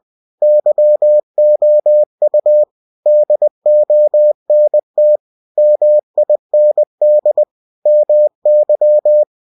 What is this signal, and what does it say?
Morse code, heard as a keyed tone, about 20 wpm, 600 Hz. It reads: YOU DONT MIND MY
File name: cw-books / winnie-the-pooh / winnie-20wpm-page-0056